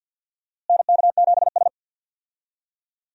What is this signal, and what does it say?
Morse code, heard as a keyed tone, about 52 wpm, 700 Hz. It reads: NK7S
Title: USA Call Signs - 4 Characters - Repeated 3x - 50wpm